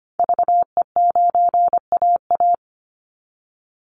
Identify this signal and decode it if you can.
Morse code, heard as a keyed tone, about 25 wpm, 700 Hz. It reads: VE9AA
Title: Call Signs Repeated 3x - 25wpm